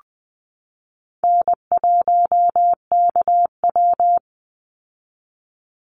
Morse code, heard as a keyed tone, about 20 wpm, 700 Hz. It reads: N1KW
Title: Easy Call Signs Repeated 2x - 20wpm